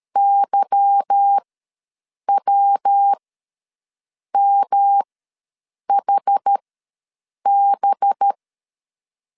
Morse code, heard as a keyed tone, about 13 wpm, spaced as for 7 wpm, 800 Hz. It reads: YWMHB
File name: LEZIONI / 9 2 ws 12wpm 3406